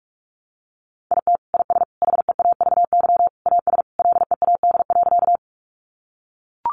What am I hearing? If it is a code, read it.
Morse code, heard as a keyed tone, about 44 wpm, 700 Hz. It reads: IT IS HEAVY AS LEAD.
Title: Sentences from Top 1100 Words 45wpm